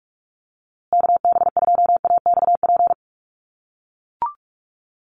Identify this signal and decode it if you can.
Morse code, heard as a keyed tone, about 44 wpm, 700 Hz. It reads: KB2AXP
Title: USA Call Signs Repeated 2x 45wpm